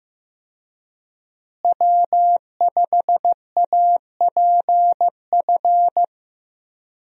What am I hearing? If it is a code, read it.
Morse code, heard as a keyed tone, about 15 wpm, 700 Hz. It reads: W5APF
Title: USA Call Signs - 5 Characters - Repeated 3x - 15wpm